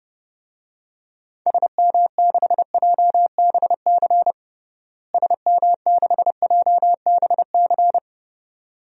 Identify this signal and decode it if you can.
Morse code, heard as a keyed tone, about 29 wpm, 700 Hz. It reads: SM6JBC SM6JBC
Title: DXCC-100 Call Signs - 6 Characters - Repeated 2x - 30wpm